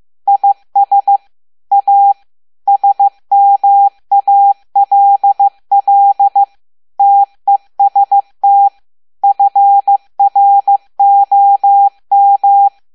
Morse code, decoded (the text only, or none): IS A SMALL TEST FROM